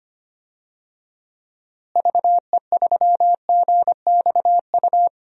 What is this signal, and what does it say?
Morse code, heard as a keyed tone, about 25 wpm, 700 Hz. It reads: VE3GXU
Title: DXCC-100 Call Signs - 6 Characters - Repeated 2x - 25wpm